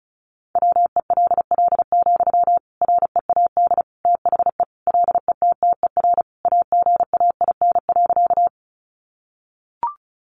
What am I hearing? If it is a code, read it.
Morse code, heard as a keyed tone, about 34 wpm, 700 Hz. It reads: WELL, READ THE LETTER AGAIN.
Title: Sentences from Top 300 Words 35wpm